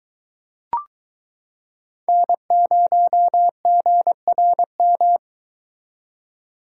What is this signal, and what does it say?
Morse code, heard as a keyed tone, about 22 wpm, 700 Hz. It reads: N0GRM